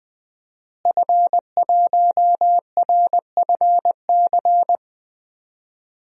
Morse code, heard as a keyed tone, about 21 wpm, 700 Hz. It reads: F1RFC
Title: DXCC-100 Call Signs - 5 Characters - Repeated 3x - 20wpm